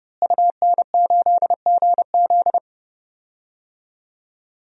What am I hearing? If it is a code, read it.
Morse code, heard as a keyed tone, about 30 wpm, 700 Hz. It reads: UN8GZ
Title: DXCC-100 Call Signs - 5 Characters - Repeated 3x - 30wpm